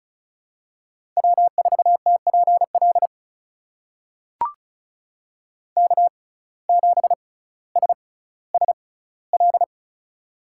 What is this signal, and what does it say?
Morse code, heard as a keyed tone, about 36 wpm, spaced as for 18 wpm, 700 Hz. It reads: W4TPL K7SSL
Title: USA Call Signs - 5 Characters - 6x Character Spacing - 35wpm